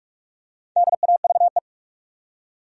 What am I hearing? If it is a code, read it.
Morse code, heard as a keyed tone, about 46 wpm, 700 Hz. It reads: DAVE
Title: USA Common Names - 45wpm